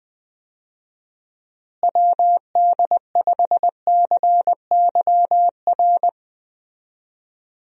Morse code, heard as a keyed tone, about 20 wpm, 700 Hz. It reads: WD5CYR
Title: USA Call Signs - 6 Characters - Repeated 3x - 20wpm